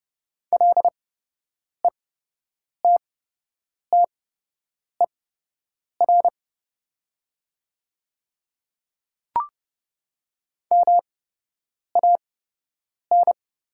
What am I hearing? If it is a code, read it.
Morse code, heard as a keyed tone, about 30 wpm, spaced as for 8 wpm, 700 Hz. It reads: LETTER MAN
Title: 8x Character Spacing - Top 500 Words - 30wpm